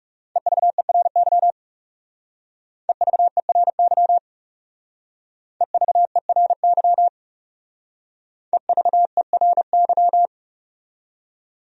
Morse code, heard as a keyed tone, about 35 wpm, 700 Hz. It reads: EVERY EVERY EVERY EVERY
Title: Speed-Racing Top 200 Words 30wpm